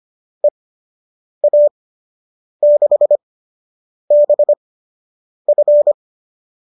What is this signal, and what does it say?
Morse code, heard as a keyed tone, about 25 wpm, spaced as for 8 wpm, 590 Hz. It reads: EA6BF